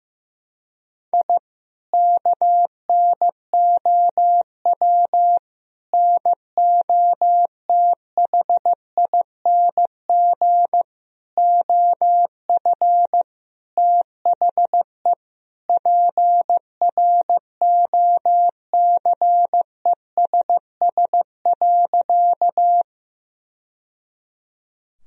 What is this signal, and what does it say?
Morse code, heard as a keyed tone, about 15 wpm, 700 Hz. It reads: I KNOW NOTHING OF THE PROCESS.
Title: Sentences from Top 1000 Words - Review Part 3 - 15wpm